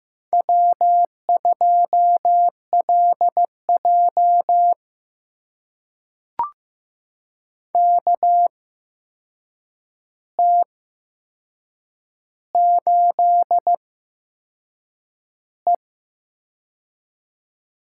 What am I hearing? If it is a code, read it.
Morse code, heard as a keyed tone, about 15 wpm, spaced as for 6 wpm, 700 Hz. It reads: W2LJ KT8E
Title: USA Call Signs - 4 Characters - 8x Character Spacing - 15wpm